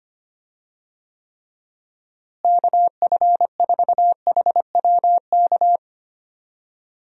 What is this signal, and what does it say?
Morse code, heard as a keyed tone, about 25 wpm, 700 Hz. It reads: KF4HWK
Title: USA Call Signs - 6 Characters - Repeated 2x - 25wpm